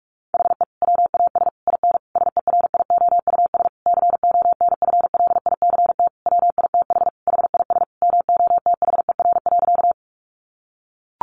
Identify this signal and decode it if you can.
Morse code, heard as a keyed tone, about 44 wpm, 700 Hz. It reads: HE WAS IN SERIOUS CONFLICT WITH HIS MOTHER.
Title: Sentences from Top 1300 Words 45wpm